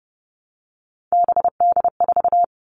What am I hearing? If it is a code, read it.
Morse code, heard as a keyed tone, about 30 wpm, 700 Hz. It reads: BD4
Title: Callsign Prefixes for Top 100 DXCC - 30wpm